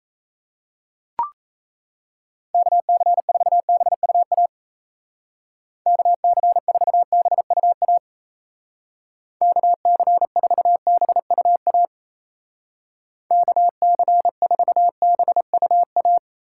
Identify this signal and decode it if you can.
Morse code, heard as a keyed tone, about 33 wpm, 700 Hz. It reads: KC4BUA KC4BUA KC4BUA KC4BUA
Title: Speed Racing - USA 2x3 Call Signs 28wpm